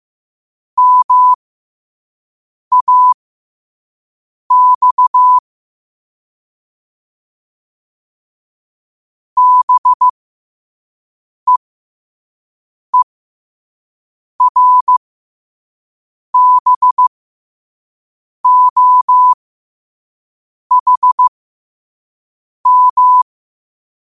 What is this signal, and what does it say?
Morse code, heard as a keyed tone, about 15 wpm, spaced as for 5 wpm, 1000 Hz. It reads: MAX BEERBOHM
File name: qotdfiles / 5wpm